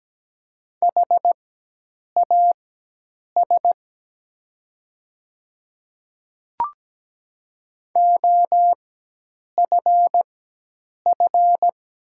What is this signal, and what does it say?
Morse code, heard as a keyed tone, about 17 wpm, spaced as for 8 wpm, 700 Hz. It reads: HAS OFF
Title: 4x Character Spacing - Top 200 Words - 17wpm